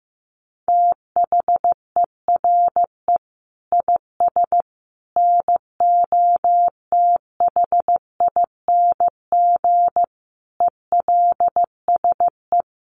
Morse code, heard as a keyed tone, about 15 wpm, 700 Hz. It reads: THERE IS NOTHING ELSE
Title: Sentences from Top 800 Words 15wpm